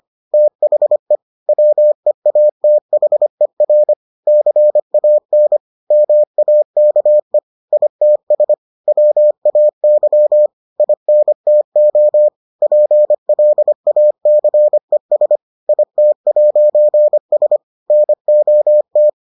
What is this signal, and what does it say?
Morse code, heard as a keyed tone, about 25 wpm, 600 Hz. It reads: THE WEATHER CAN MAKE ITS WAY INTO PLACES IT'S NOT